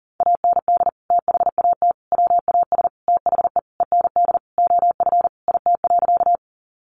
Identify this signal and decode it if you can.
Morse code, heard as a keyed tone, about 40 wpm, 700 Hz. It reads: AND THAT WAS THE END OF IT.
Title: Sentences from Top 200 Words 40wpm